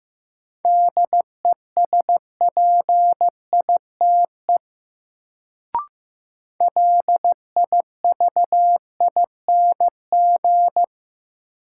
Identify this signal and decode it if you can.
Morse code, heard as a keyed tone, about 15 wpm, 700 Hz. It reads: DESPITE LIVING
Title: Sets of 4 Words - New Word Drill - 15wpm